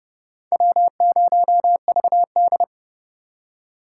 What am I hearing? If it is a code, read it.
Morse code, heard as a keyed tone, about 27 wpm, 700 Hz. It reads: W0VD